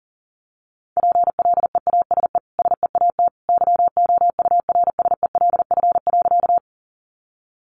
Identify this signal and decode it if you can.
Morse code, heard as a keyed tone, about 37 wpm, 700 Hz. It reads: PLEASE SEAT YOURSELF.
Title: Sentences from Top 1500 Words 40wpm